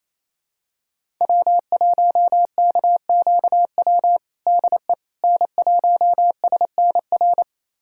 Morse code, heard as a keyed tone, about 28 wpm, 700 Hz. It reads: W1KQW DE N1SNR
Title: USA Callsign DE Callsign - 5 Characters - 28wpm